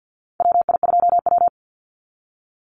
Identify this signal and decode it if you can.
Morse code, heard as a keyed tone, about 50 wpm, 700 Hz. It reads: WI2W
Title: NA SOTA Chaser Callsigns 50wpm